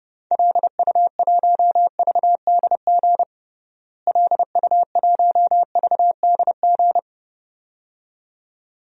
Morse code, heard as a keyed tone, about 31 wpm, 700 Hz. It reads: LU1VDG LU1VDG
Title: DXCC-100 Call Signs - 6 Characters - Repeated 2x - 30wpm